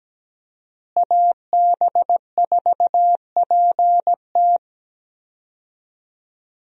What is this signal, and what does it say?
Morse code, heard as a keyed tone, about 17 wpm, 700 Hz. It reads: AB4PT
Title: USA Call Signs - 5 Characters - Repeated 3x - 17wpm